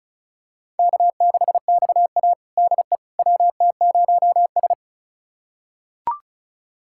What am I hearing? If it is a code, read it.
Morse code, heard as a keyed tone, about 34 wpm, 700 Hz. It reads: K6XA DE WT0S
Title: USA Callsign DE Callsign - 4 Characters - 35wpm